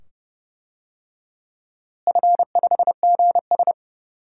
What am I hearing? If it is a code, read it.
Morse code, heard as a keyed tone, about 29 wpm, 700 Hz. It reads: F5GS
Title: DXCC-100 Call Signs - 4 Characters - Repeated 2x - 30wpm